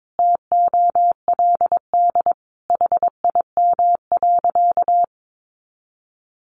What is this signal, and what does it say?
Morse code, heard as a keyed tone, about 22 wpm, 700 Hz. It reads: TOLD HIM.
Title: Sentences from Top 1200 Words 22wpm